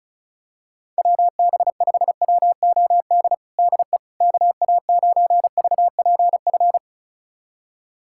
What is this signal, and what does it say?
Morse code, heard as a keyed tone, about 36 wpm, 700 Hz. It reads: WB5WOD DE KA9VPF
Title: USA Callsign DE Callsign - 6 Characters - 35wpm